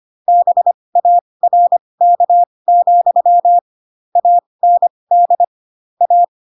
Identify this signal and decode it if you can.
Morse code, heard as a keyed tone, about 25 wpm, spaced as for 20 wpm, 700 Hz. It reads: BARK, AND A
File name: Pinocchio / Pinocchio ch8 15 25 ik1hge